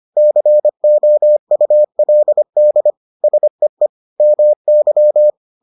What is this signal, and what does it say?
Morse code, heard as a keyed tone, about 25 wpm, 600 Hz. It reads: COULD SEE MY